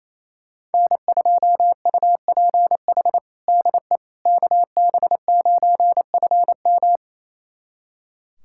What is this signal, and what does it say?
Morse code, heard as a keyed tone, about 28 wpm, 700 Hz. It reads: N2UPH DE KB9FM
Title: USA Callsign DE Callsign - 5 Characters - 28wpm